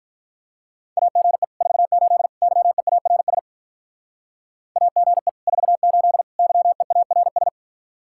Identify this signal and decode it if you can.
Morse code, heard as a keyed tone, about 43 wpm, 700 Hz. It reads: AGE 48 YEARS AGE 48 YEARS
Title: Speed-Racing Common QSO Phrases Encore - 35wpm